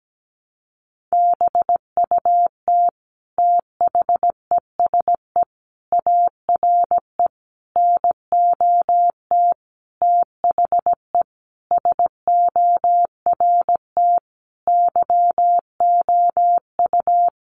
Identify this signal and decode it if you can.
Morse code, heard as a keyed tone, about 17 wpm, 700 Hz. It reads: BUT THESE ARE NOT THE SORT YOU
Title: Sentences from Top 1300 Words 17wpm